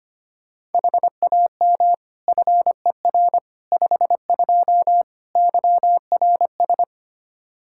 Morse code, heard as a keyed tone, about 25 wpm, 700 Hz. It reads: HAM FER 52 YRS
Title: Practice QSO - Exchange 3 - More Info - 25wpm